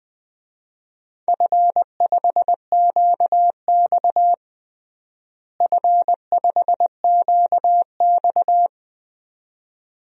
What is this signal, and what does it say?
Morse code, heard as a keyed tone, about 21 wpm, 700 Hz. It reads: F5QX F5QX
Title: DXCC-100 Call Signs - 4 Characters - Repeated 2x - 20wpm